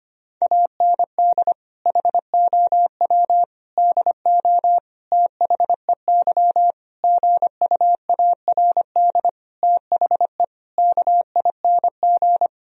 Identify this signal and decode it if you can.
Morse code, heard as a keyed tone, about 25 wpm, 700 Hz. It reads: AND HOW DO THEY GUARD THE KING
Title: Sentences from Top 1000 Words - Review Part 4 - 25wpm